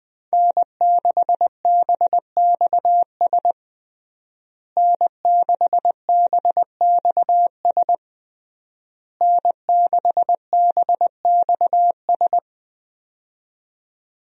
Morse code, heard as a keyed tone, about 20 wpm, 700 Hz. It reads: N6BXS N6BXS N6BXS
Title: USA Call Signs - 5 Characters - Repeated 3x - 20wpm